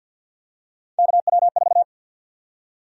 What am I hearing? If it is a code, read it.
Morse code, heard as a keyed tone, about 53 wpm, 700 Hz. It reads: KW4
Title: USA Call Sign Prefixes - 50wpm